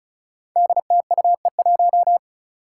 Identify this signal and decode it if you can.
Morse code, heard as a keyed tone, about 36 wpm, 700 Hz. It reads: DTUE1
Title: Five Challenging Letter-Number Combinations 35wpm